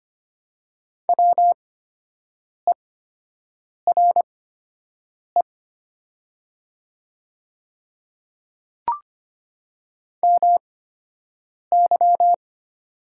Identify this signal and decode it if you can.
Morse code, heard as a keyed tone, about 25 wpm, spaced as for 7 wpm, 700 Hz. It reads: WERE MY
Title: 8x Character Spacing - Top 100 Words - 25wpm